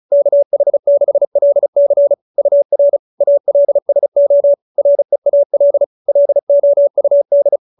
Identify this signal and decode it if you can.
Morse code, heard as a keyed tone, about 34 wpm, 590 Hz. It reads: KH6LC UR ALSO REAL LOUD